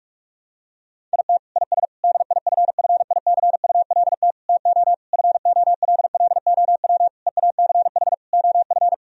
Morse code, heard as a keyed tone, about 46 wpm, 700 Hz. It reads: IT IS DIFFICULT TO FOLLOW EACH OF
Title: Sentences from Top 1000 Words - Review Part 1 - 45wpm